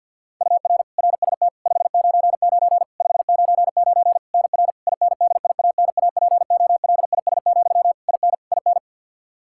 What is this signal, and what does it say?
Morse code, heard as a keyed tone, about 51 wpm, 700 Hz. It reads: UR RST 599 599 NR INDIANAPOLIS, IN IN
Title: Practice QSO - Exchange 1 - Beginning - 50wpm